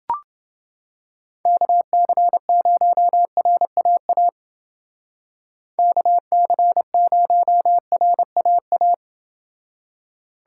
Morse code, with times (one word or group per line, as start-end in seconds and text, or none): 1.5-4.3 KC0RAA
5.8-8.9 KC0RAA